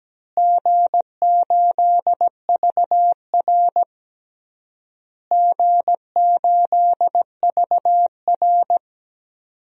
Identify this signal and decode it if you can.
Morse code, heard as a keyed tone, about 17 wpm, 700 Hz. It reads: G8VR G8VR
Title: DXCC-100 Call Signs - 4 Characters - Repeated 3x - 17wpm